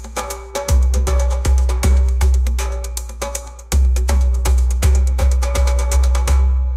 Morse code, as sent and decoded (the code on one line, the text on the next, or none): none